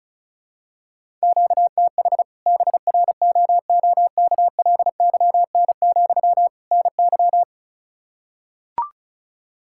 Morse code, heard as a keyed tone, about 36 wpm, 700 Hz. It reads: QTH BROOKLYN, NY
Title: Practice QSO - Exchange 1 - Beginning - 35wpm